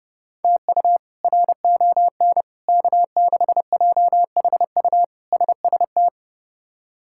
Morse code, heard as a keyed tone, about 31 wpm, 700 Hz. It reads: TU RON K6JHU SST
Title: K1USN SST QSO Practice - 30wpm